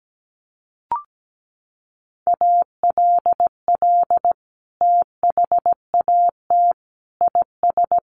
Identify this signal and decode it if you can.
Morse code, heard as a keyed tone, about 17 wpm, 700 Hz. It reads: ALL THAT IS
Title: Sentences from Top 1100 Words 17wpm